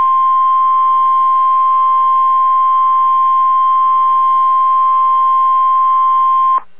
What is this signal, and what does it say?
Emergency Alert System (SAME) tones (the 1050 Hz attention tone). It is heard from 1.2 to 5.9 s.